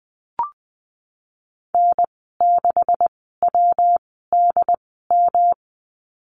Morse code, heard as a keyed tone, about 20 wpm, spaced as for 15 wpm, 700 Hz. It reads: N6WDM
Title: 147 Lesson 24g - Letter y - Call Signs 20wpm